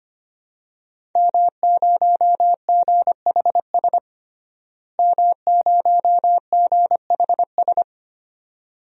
Morse code, heard as a keyed tone, about 25 wpm, 700 Hz. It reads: M0GHS M0GHS
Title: DXCC-100 Call Signs - 5 Characters - Repeated 2x - 25wpm